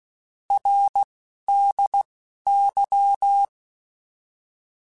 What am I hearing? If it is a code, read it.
Morse code, heard as a keyed tone, about 16 wpm, 800 Hz. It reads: R D Y
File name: Abreviations / Abreviations mp3s / Abreviations@10